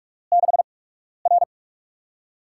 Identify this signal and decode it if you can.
Morse code, heard as a keyed tone, about 45 wpm, spaced as for 12 wpm, 700 Hz. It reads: 6R
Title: USA Call Signs - 8x Character Spacing - 45wpm